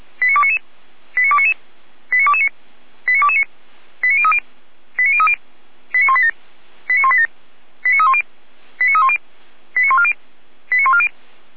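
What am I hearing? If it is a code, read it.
Five-tone selective call (ZVEI), heard as quick runs of five tones, tones of about 70 ms. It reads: 89299 89299 89229 89229 89030 89030 89187 89187 89310 89310 89140 89140
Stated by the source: ZVEI five-tone selective call received via SDR, Feuerwehr Orlamünde.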